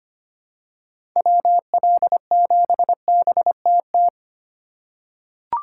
Morse code, heard as a keyed tone, about 25 wpm, 700 Hz. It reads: WL7BTT
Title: USA Call Signs - 3x Character Spacing - 25wpm